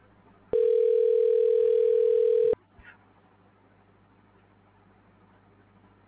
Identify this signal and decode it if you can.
Ringback tone (North America), 440 + 480 Hz.